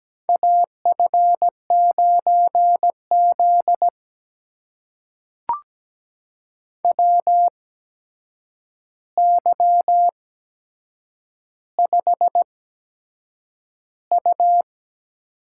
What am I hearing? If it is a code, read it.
Morse code, heard as a keyed tone, about 17 wpm, spaced as for 7 wpm, 700 Hz. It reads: AF9Z WY5U
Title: USA Call Signs - 4 Characters - 8x Character Spacing - 17wpm